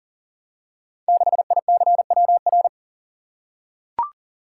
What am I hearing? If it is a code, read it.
Morse code, heard as a keyed tone, about 39 wpm, 700 Hz. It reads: 6ICWR
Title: Code Groups - 5 Letter:Numbers - 40wpm